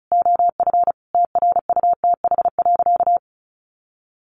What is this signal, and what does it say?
Morse code, heard as a keyed tone, about 34 wpm, 700 Hz. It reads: OF TRUTH.
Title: Sentences from Top 1500 Words 35wpm